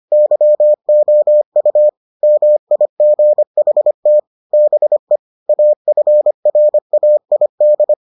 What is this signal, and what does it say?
Morse code, heard as a keyed tone, about 25 wpm, 600 Hz. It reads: YOU MIGHT BE AFRAID